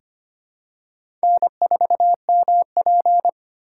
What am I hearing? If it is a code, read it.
Morse code, heard as a keyed tone, about 25 wpm, 700 Hz. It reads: N4MP
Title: USA Call Signs - 4 Characters - 3x Character Spacing - 25wpm